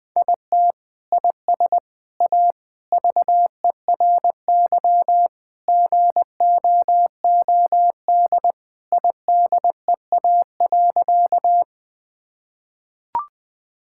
Morse code, heard as a keyed tone, about 21 wpm, 700 Hz. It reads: IT IS A VERY GOOD IDEA.
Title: Sentences with 3-Word Phrases - 20wpm